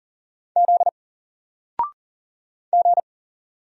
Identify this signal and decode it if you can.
Morse code, heard as a keyed tone, about 40 wpm, 700 Hz. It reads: Z G
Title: Single Letter-Number 40wpm